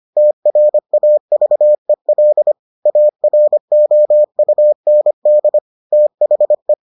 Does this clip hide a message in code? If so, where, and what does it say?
Yes: Morse code from 0.2 to 6.7 s, heard as a keyed tone, about 25 wpm, 600 Hz. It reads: TRAVEL AROUND THE